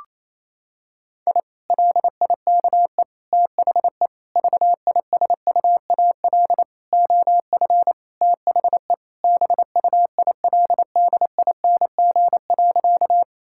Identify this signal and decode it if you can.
Morse code, heard as a keyed tone, about 28 wpm, 700 Hz. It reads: I LIKE THE VISUAL OF THE BUILDING.
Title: Sentences from Top 1000 Words - Review Part 6 - 28wpm